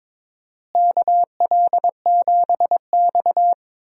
Morse code, heard as a keyed tone, about 22 wpm, 700 Hz. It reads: KL7X
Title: DXCC-100 Call Signs - 4 Characters - Repeated 2x - 22wpm